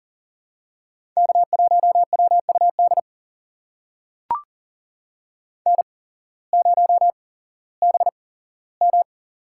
Morse code, heard as a keyed tone, about 40 wpm, spaced as for 19 wpm, 700 Hz. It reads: K1WUD N0BM
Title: USA Call Signs - 8x Character Spacing - 40wpm